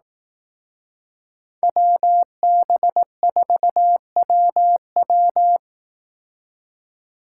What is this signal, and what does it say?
Morse code, heard as a keyed tone, about 18 wpm, 700 Hz. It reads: WB4WW
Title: Speed Racing - USA 2x2 Call Signs 15wpm